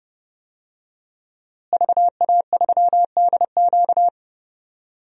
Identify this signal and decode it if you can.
Morse code, heard as a keyed tone, about 31 wpm, 700 Hz. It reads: VA3DQ
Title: DXCC-100 Call Signs - 5 Characters - Repeated 3x - 30wpm